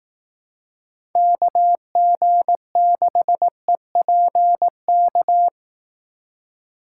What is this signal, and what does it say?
Morse code, heard as a keyed tone, about 18 wpm, 700 Hz. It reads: KG6EPK
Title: Speed Racing - USA 2x3 Call Signs 15wpm